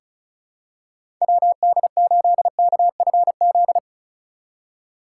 Morse code, heard as a keyed tone, about 36 wpm, 700 Hz. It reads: WD8KFZ
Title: USA Call Signs - 3x Character Spacing - 35wpm